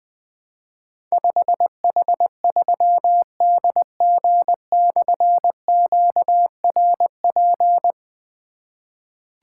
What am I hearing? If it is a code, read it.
Morse code, heard as a keyed tone, about 20 wpm, 700 Hz. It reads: 5H3DG/QRP
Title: Difficult Call Signs 20wpm